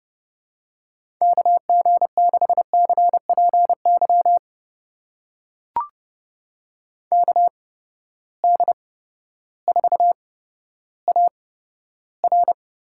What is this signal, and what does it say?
Morse code, heard as a keyed tone, about 30 wpm, spaced as for 14 wpm, 700 Hz. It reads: KG6CPY KD4AR
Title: USA Call Signs - 6 Characters - 8x Character Spacing - 30wpm